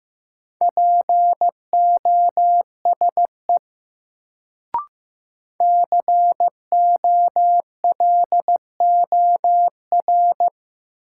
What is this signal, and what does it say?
Morse code, heard as a keyed tone, about 15 wpm, 700 Hz. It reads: POSE COLOR
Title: Sentences from Top 1000 Words - Review Part 2 - New Word Drill - 15wpm